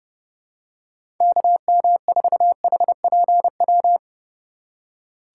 Morse code, heard as a keyed tone, about 32 wpm, 700 Hz. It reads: KM4HPW
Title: USA Call Signs - 6 Characters - 4x Character Spacing - 30wpm